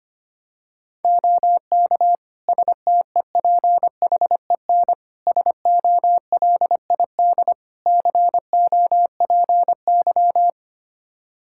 Morse code, heard as a keyed tone, about 25 wpm, 700 Hz. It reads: OK STEPHEN SOLID COPY
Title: K1USN SST QSO Practice 25wpm